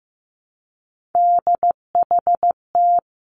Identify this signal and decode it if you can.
Morse code, heard as a keyed tone, about 15 wpm, 700 Hz. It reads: DHT
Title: Three Challenging Letter-Number Combinations 15wpm